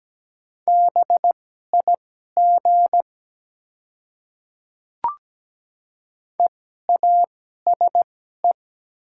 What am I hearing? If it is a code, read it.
Morse code, heard as a keyed tone, about 17 wpm, spaced as for 12 wpm, 700 Hz. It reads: BIG EASE
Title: 2x Character Spacing - Top 300 Words - 17wpm